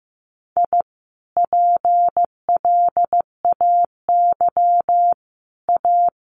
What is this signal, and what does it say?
Morse code, heard as a keyed tone, about 15 wpm, 700 Hz. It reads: I PLAY A
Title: Sentences from Top 1100 Words 15wpm